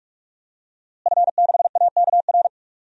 Morse code, heard as a keyed tone, about 47 wpm, 700 Hz. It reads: F6AKR
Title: DXCC-100 Call Signs - 5 Characters - Repeated 2x - 45wpm